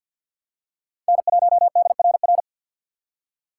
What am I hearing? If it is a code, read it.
Morse code, heard as a keyed tone, about 51 wpm, 700 Hz. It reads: N1DRR
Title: USA Call Signs - 5 Characters - 3x Character Spacing - 50wpm